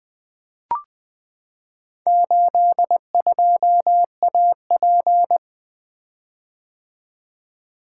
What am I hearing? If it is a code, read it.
Morse code, heard as a keyed tone, about 20 wpm, 700 Hz. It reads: 82AP
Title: Four Letter-Number Combinations - 20wpm